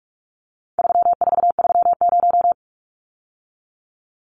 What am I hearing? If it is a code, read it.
Morse code, heard as a keyed tone, about 44 wpm, 700 Hz. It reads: 3430
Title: Numbers - 4-Digit 45wpm